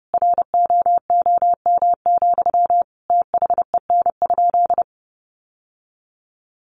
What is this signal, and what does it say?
Morse code, heard as a keyed tone, about 30 wpm, 700 Hz. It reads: ROOM, THEN?
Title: Sentences from Top 600 Words 30wpm